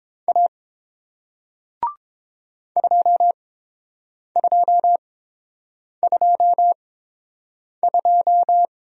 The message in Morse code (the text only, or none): A 2222